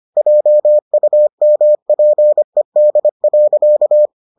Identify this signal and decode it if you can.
Morse code, heard as a keyed tone, about 25 wpm, 600 Hz. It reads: JUMPED.